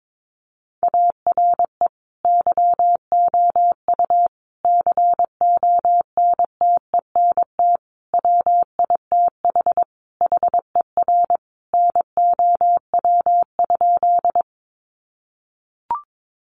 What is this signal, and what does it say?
Morse code, heard as a keyed tone, about 22 wpm, 700 Hz. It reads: ARE YOU CONTENT WITH HER NOW?
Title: Sentences from Top 900 Words 22wpm